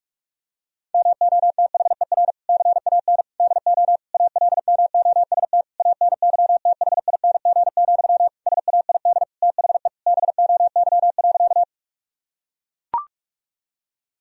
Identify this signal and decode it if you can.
Morse code, heard as a keyed tone, about 43 wpm, 700 Hz. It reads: MOTHER CAN DO ALMOST ANYTHING, SAID THE BOY.
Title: Sentences from Top 1000 Words - Review Part 4 - 45wpm